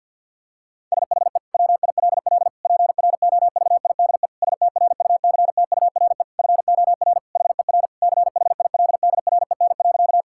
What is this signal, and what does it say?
Morse code, heard as a keyed tone, about 47 wpm, 700 Hz. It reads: SHE WILL PROVIDE STRUCTURE FOR HER CHILDREN.
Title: Sentences from Top 1000 Words - Review Part 6 - 50wpm